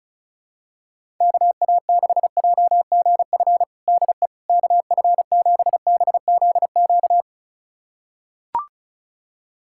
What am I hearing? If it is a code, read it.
Morse code, heard as a keyed tone, about 36 wpm, 700 Hz. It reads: KA6JGF DE KF7BZQ
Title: USA Callsign DE Callsign - 6 Characters - 35wpm